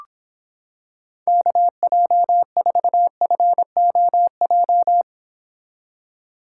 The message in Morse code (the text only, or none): KJ4FOJ